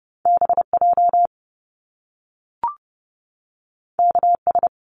Morse code, heard as a keyed tone, about 30 wpm, 700 Hz. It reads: BJ KS
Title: Top Two Letter Combinations in Words 30wpm